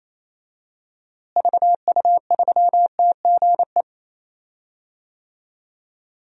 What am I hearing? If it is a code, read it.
Morse code, heard as a keyed tone, about 27 wpm, 700 Hz. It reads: VU3TGE
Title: DXCC-100 Call Signs - 6 Characters - Repeated 2x - 28wpm